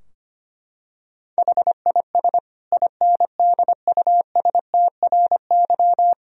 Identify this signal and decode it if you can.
Morse code, heard as a keyed tone, about 25 wpm, 700 Hz. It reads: HIS INDUSTRY
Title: Sentences from Top 1000 Words - Review Part 4 - 25wpm